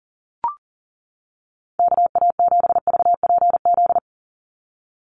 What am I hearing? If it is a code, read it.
Morse code, heard as a keyed tone, about 40 wpm, 700 Hz. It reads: KA7VPZ
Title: USA Call Signs 40wpm